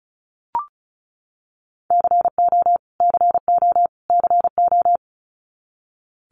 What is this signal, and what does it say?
Morse code, heard as a keyed tone, about 34 wpm, 700 Hz. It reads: CO CO CO
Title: US States Repeated 3x 35wpm